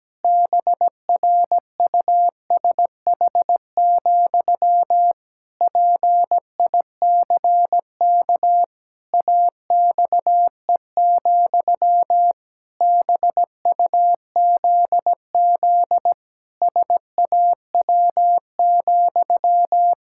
Morse code, heard as a keyed tone, about 17 wpm, 700 Hz. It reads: BRUSH, PICK AXE, BUZZ SAW,
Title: Pangram Challenge v2 - 17wpm